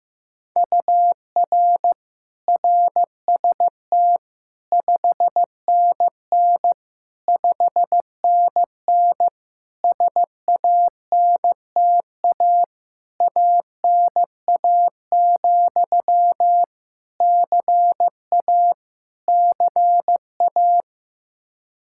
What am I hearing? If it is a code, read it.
Morse code, heard as a keyed tone, about 15 wpm, 700 Hz. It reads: UR RST 5NN 5NN SANTA ANA, CA CA
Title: Practice QSO - Exchange 1 - Beginning - 15wpm